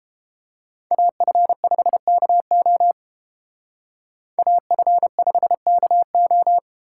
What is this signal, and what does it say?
Morse code, heard as a keyed tone, about 28 wpm, 700 Hz. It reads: AF5KO AF5KO